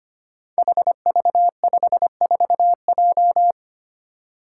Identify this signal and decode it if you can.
Morse code, heard as a keyed tone, about 25 wpm, 700 Hz. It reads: HV54J
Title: Five Challenging Letter-Number Combinations 25wpm